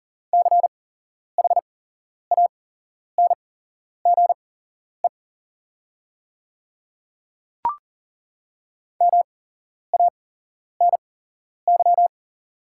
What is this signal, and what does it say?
Morse code, heard as a keyed tone, about 40 wpm, spaced as for 11 wpm, 700 Hz. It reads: CHANGE MANY
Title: 8x Character Spacing - Top 200 Words - 40wpm